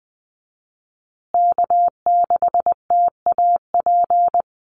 Morse code, heard as a keyed tone, about 20 wpm, 700 Hz. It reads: K6TAP
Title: USA Call Signs Repeated 2x 20wpm